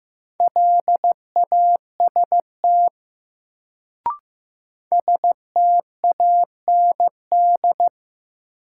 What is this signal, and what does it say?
Morse code, heard as a keyed tone, about 15 wpm, 700 Hz. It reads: LAST STAND
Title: Sentences from Top 300 Words - New Word Drill - 15wpm